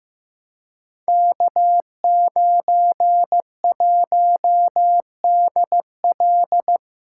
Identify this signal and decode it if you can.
Morse code, heard as a keyed tone, about 15 wpm, 700 Hz. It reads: K91DL
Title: Code Groups - 5 Letter:Numbers - 15wpm